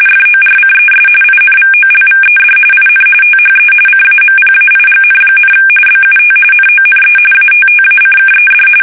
RTTY, about 45.45 baud. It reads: FORCES INVOLVED IN THE CAMPAIGN, BUT NONE ATTACKED
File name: fsk850 / USS-Massachusetts